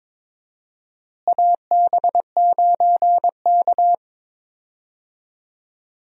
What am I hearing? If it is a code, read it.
Morse code, heard as a keyed tone, about 22 wpm, 700 Hz. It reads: AB9K
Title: USA Call Signs - 4 Characters - Repeated 2x - 22wpm